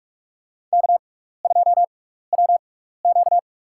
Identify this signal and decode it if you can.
Morse code, heard as a keyed tone, about 47 wpm, spaced as for 16 wpm, 700 Hz. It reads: K2WQ
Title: USA Call Signs - 4 Characters - 6x Character Spacing - 45wpm